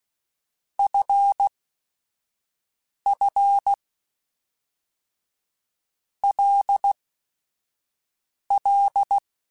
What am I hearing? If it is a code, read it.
Morse code, heard as a keyed tone, about 16 wpm, 800 Hz. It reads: F F L L
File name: Basic Sets / Basic Letter Groups Collective7 / LFCYQ@5